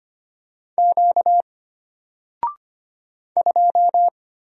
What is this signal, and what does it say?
Morse code, heard as a keyed tone, about 25 wpm, 700 Hz. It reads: Q 2
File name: Single-Letter-Number-25wpm